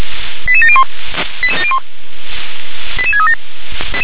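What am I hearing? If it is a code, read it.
Five-tone selective call (ZVEI), heard as quick runs of five tones, tones of about 70 ms. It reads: 88701 88637
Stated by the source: ZVEI five-tone selective call received on an SDR.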